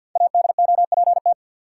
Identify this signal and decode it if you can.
Morse code, heard as a keyed tone, about 47 wpm, 700 Hz. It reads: ADOPT
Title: Five Letter Words - 50wpm